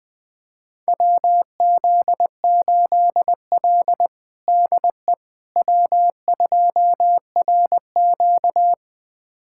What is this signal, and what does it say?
Morse code, heard as a keyed tone, about 20 wpm, 700 Hz. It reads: WZ8L DE W2RQ
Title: USA Callsign DE Callsign - 4 Characters - 20wpm